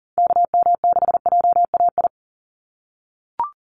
Morse code, heard as a keyed tone, about 40 wpm, 700 Hz. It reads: KM6JAI
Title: USA Call Signs 40wpm